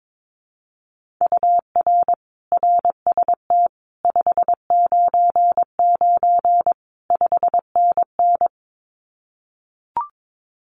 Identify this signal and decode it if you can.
Morse code, heard as a keyed tone, about 22 wpm, 700 Hz. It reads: UR RST 599 5NN
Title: Common QSO Phrases Encore 22wpm